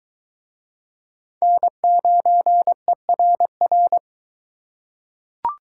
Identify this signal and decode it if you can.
Morse code, heard as a keyed tone, about 21 wpm, 700 Hz. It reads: N9ERR